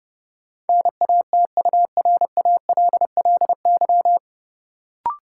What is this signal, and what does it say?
Morse code, heard as a keyed tone, about 30 wpm, 700 Hz. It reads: NATURALLY
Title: Sentences from Top 1300 Words - New Word Drill - 30wpm